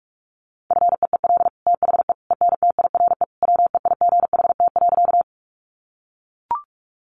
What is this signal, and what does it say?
Morse code, heard as a keyed tone, about 44 wpm, 700 Hz. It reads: FEEL THE ENTIRE WEIGHT.
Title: Sentences from Top 800 Words 45wpm